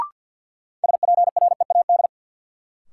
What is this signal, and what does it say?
Morse code, heard as a keyed tone, about 52 wpm, 700 Hz. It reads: SPREAD
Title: Sentences from Top 1200 Words - New Word Drill - 50wpm